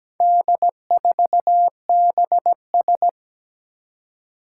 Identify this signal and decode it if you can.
Morse code, heard as a keyed tone, about 17 wpm, 700 Hz. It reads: D4BS
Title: Four Challenging Letter-Number Combinations 17wpm